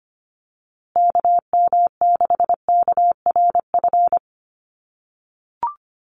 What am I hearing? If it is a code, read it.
Morse code, heard as a keyed tone, about 25 wpm, 700 Hz. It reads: KM6KRF
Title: USA Call Signs Repeated 3x 25wpm